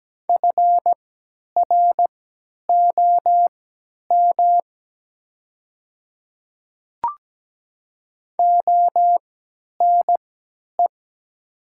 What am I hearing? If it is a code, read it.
Morse code, heard as a keyed tone, about 17 wpm, spaced as for 10 wpm, 700 Hz. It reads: FROM ONE
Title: 3x Character Spacing - Top 100 Words - 17wpm